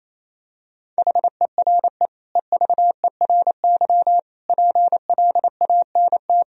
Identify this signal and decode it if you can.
Morse code, heard as a keyed tone, about 28 wpm, 700 Hz. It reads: HERE EVERY PLANT
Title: Sentences from Top 1000 Words - Review Part 1 - 28wpm